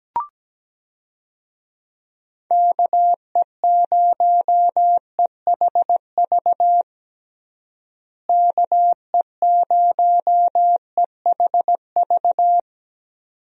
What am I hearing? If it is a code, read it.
Morse code, heard as a keyed tone, about 17 wpm, 700 Hz. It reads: KE0EHV KE0EHV
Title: USA Call Signs - 6 Characters - Repeated 2x - 17wpm